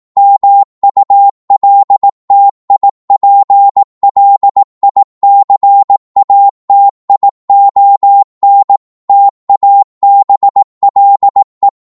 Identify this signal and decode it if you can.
Morse code, heard as a keyed tone, about 18 wpm, 800 Hz. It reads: MULTIPLICATION TABLE